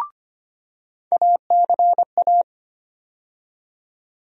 Morse code, heard as a keyed tone, about 25 wpm, 700 Hz. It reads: ACA
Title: USA Call Sign Postfixes - 3 Letters 25wpm